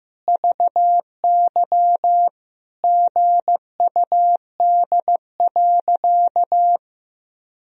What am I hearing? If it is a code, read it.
Morse code, heard as a keyed tone, about 15 wpm, 700 Hz. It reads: VY GUD.
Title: Sentences from Top 200 Words and Abbreviations 15wpm